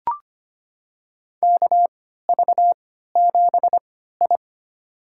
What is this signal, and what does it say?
Morse code, heard as a keyed tone, about 25 wpm, spaced as for 14 wpm, 700 Hz. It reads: KV7I
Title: USA Call Signs - 4 Characters - 3x Character Spacing - 25wpm